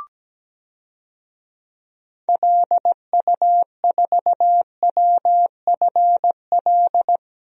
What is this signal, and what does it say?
Morse code, heard as a keyed tone, about 17 wpm, 700 Hz. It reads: LU4WFL
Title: DXCC-100 Call Signs - 6 Characters - Repeated 2x - 17wpm